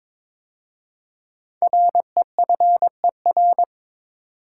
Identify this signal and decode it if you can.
Morse code, heard as a keyed tone, about 22 wpm, 700 Hz. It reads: REFER
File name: 5 Letter Words - First 3 Letters Given - 22wpm